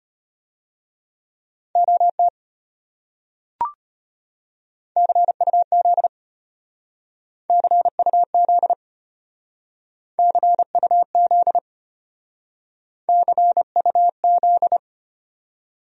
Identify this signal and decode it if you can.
Morse code, heard as a keyed tone, about 32 wpm, 700 Hz. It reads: OT CUZ CUZ CUZ CUZ